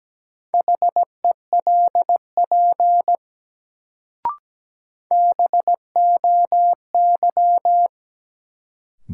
Morse code, heard as a keyed tone, about 17 wpm, 700 Hz. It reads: HELP BOY
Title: Sentences from Top 200 Words - New Word Drill - 17wpm